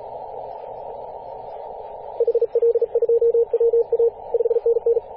Morse code, heard as a keyed tone, about 35 wpm, 470 Hz, with radio noise. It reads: HL2WA 5NN
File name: posts / 2016-apr-22-2153z-kh8-kc0w-18-080mhz-cw